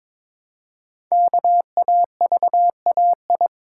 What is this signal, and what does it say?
Morse code, heard as a keyed tone, about 22 wpm, 700 Hz. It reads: KAVAI
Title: Code Groups - 5 Letters - 22wpm